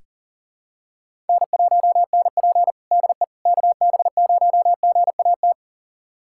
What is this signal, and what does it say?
Morse code, heard as a keyed tone, about 40 wpm, 700 Hz. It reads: N1NP DE KB0GAT
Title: USA Callsign DE Callsign 40wpm